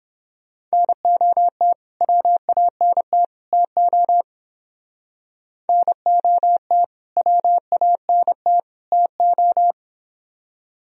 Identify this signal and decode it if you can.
Morse code, heard as a keyed tone, about 31 wpm, 700 Hz. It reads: NOT WANT TO NOT WANT TO